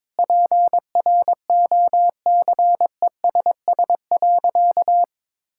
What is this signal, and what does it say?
Morse code, heard as a keyed tone, about 22 wpm, 700 Hz. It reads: PROCESS.
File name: practice / 22wpm / Sentences with 2-Word Phrases - 22wpm